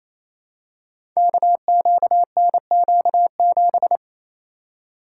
Morse code, heard as a keyed tone, about 28 wpm, 700 Hz. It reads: KQNQ7
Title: Code Groups - 5 Letter:Numbers - 28wpm